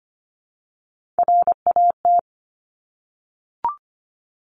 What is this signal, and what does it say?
Morse code, heard as a keyed tone, about 25 wpm, 700 Hz. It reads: RAT
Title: Top Three Letter Combinations in Words 25wpm